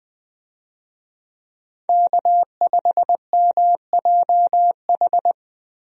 Morse code, heard as a keyed tone, about 20 wpm, 700 Hz. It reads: K5MJH
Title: USA Call Signs - 5 Characters - Repeated 2x - 20wpm